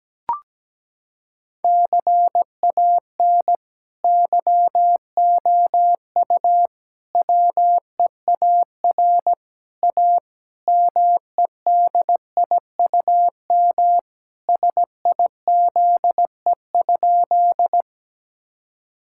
Morse code, heard as a keyed tone, about 17 wpm, 700 Hz. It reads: CAN YOU WEAR A MEDIUM SIZE?
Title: Sentences from Top 1000 Words - Review Part 6 - 17wpm